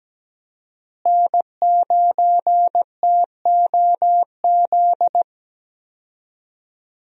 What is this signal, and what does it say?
Morse code, heard as a keyed tone, about 17 wpm, 700 Hz. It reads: N9TOZ
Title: USA Call Signs - 5 Characters - Repeated 3x - 17wpm